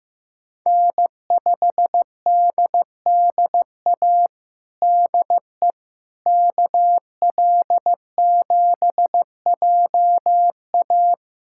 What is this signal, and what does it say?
Morse code, heard as a keyed tone, about 15 wpm, 700 Hz. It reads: N5DDA DE KL7JA
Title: USA Callsign DE Callsign - 5 Characters - 15wpm